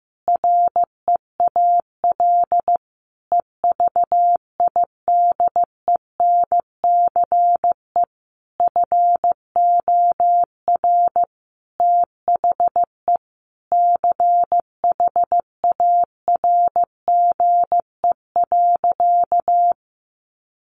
Morse code, heard as a keyed tone, about 15 wpm, 700 Hz. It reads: REAL EVIDENCE FOR THE CHARGE.
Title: Sentences from Top 1300 Words 15wpm